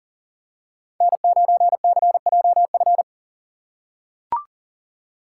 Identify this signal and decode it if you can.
Morse code, heard as a keyed tone, about 39 wpm, 700 Hz. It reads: N9CJF
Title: USA Call Signs - 5 Characters - 3x Character Spacing - 40wpm